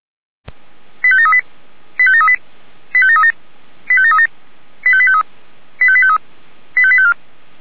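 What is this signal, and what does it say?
Five-tone selective call (ZVEI), heard as quick runs of five tones, tones of about 70 ms. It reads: 85738 85738 85737 85737 85853 85853 85854
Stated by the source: ZVEI five-tone call 85825 received on an SDR.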